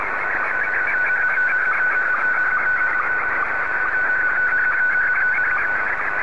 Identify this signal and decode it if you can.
SSTV (slow-scan television), Scottie 1.